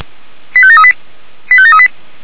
Five-tone selective call (ZVEI), heard as quick runs of five tones, tones of about 70 ms. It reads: 85738 85738